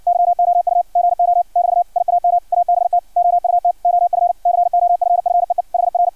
Morse code, heard as a keyed tone, about 56 wpm, 700 Hz. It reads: YOU GO <BT> IAM ABT OUT OF COFFEE HR
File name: archived / CRSnet